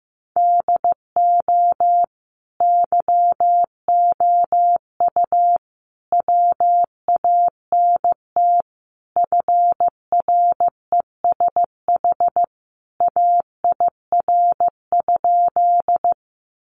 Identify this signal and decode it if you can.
Morse code, heard as a keyed tone, about 15 wpm, 700 Hz. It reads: DO YOU WANT FRESH AIR?
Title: Sentences from Top 1200 Words 15wpm